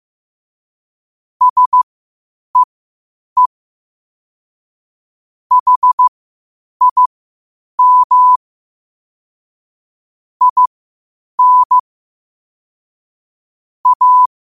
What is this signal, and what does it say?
Morse code, heard as a keyed tone, about 15 wpm, spaced as for 8 wpm, 1000 Hz. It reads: SEE HIM IN A
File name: qotdfiles / 8wpm